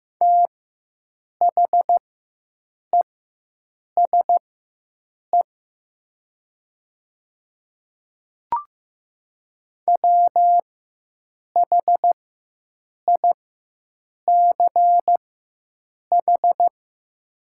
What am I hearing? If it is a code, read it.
Morse code, heard as a keyed tone, about 15 wpm, spaced as for 7 wpm, 700 Hz. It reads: THESE WHICH
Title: 4x Character Spacing - Top 100 Words - 15wpm